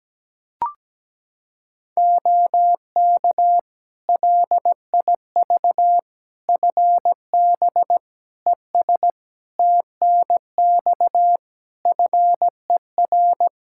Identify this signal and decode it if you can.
Morse code, heard as a keyed tone, about 17 wpm, 700 Hz. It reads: OK LIV FB ES TNX FER
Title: Practice QSO - Exchange 2 - Info - 17wpm